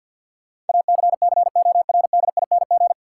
Morse code, heard as a keyed tone, about 53 wpm, 700 Hz. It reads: ACCORDING
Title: Sets of 2 Words - New Word Drill - 50wpm